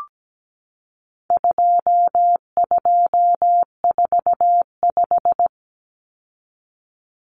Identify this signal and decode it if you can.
Morse code, heard as a keyed tone, about 17 wpm, 700 Hz. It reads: 2245
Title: Numbers - 4-Digit 17wpm